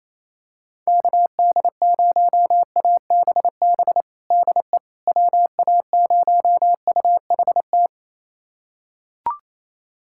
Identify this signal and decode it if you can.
Morse code, heard as a keyed tone, about 28 wpm, 700 Hz. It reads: KD0ABB DE WA0UHT
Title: USA Callsign DE Callsign - 6 Characters - 28wpm